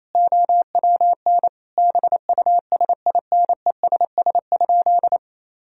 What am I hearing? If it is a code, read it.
Morse code, heard as a keyed tone, about 28 wpm, 700 Hz. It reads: OWN BUSINESS?
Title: Sentences from Top 1000 Words - Review Part 4 - 28wpm